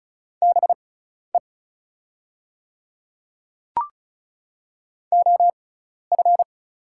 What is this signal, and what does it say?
Morse code, heard as a keyed tone, about 35 wpm, spaced as for 12 wpm, 690 Hz. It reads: BE OF
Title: 6x Character Spacing - Top 200 Words - 35wpm